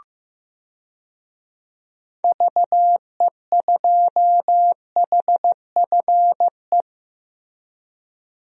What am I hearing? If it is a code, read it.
Morse code, heard as a keyed tone, about 16 wpm, 700 Hz. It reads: VE2HFE
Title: DXCC-100 Call Signs - 6 Characters - Repeated 2x - 15wpm